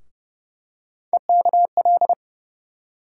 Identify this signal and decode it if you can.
Morse code, heard as a keyed tone, about 31 wpm, 700 Hz. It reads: EKL
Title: USA Call Sign Postfixes - 3 Letters 30wpm